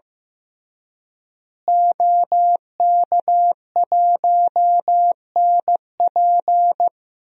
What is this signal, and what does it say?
Morse code, heard as a keyed tone, about 15 wpm, 700 Hz. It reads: OK1NP
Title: DXCC-100 Call Signs - 5 Characters - Repeated 3x - 15wpm